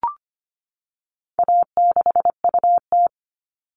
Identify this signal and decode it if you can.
Morse code, heard as a keyed tone, about 25 wpm, 700 Hz. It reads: A6UT
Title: All Cut Numbers - 4-Digit 25wpm